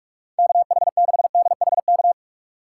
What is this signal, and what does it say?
Morse code, heard as a keyed tone, about 41 wpm, 700 Hz. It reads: KH6DHK
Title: K1USN SST QSO Practice - 45wpm